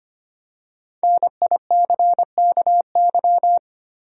Morse code, heard as a keyed tone, about 25 wpm, 700 Hz. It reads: NICKY
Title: USA Common Names - Max 5 Letters - 25wpm